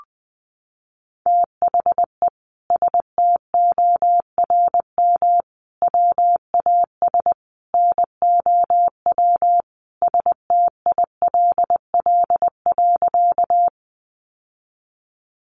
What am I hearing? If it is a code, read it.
Morse code, heard as a keyed tone, about 20 wpm, 700 Hz. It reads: THE STORM WAS NOW STILL.
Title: Sentences from Top 1500 Words 20wpm